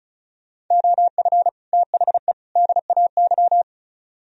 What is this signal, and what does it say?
Morse code, heard as a keyed tone, about 32 wpm, 700 Hz. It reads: OF THE DAY
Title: Speed-Racing - Sets of 3 Words 35wpm